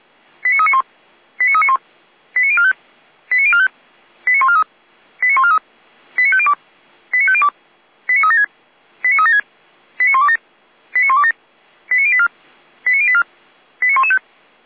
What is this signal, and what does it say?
Five-tone selective call (ZVEI), heard as quick runs of five tones, tones of about 70 ms. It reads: 89391 89391 89045 89045 89143 89143 89502 89502 89376 89376 89127 89127 89084 89084 89115